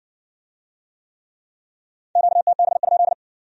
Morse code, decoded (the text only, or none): QTB?